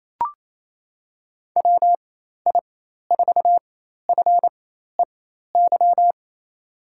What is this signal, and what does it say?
Morse code, heard as a keyed tone, about 28 wpm, spaced as for 13 wpm, 700 Hz. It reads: WI4FEY
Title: USA Call Signs - 4x Character Spacing - 28wpm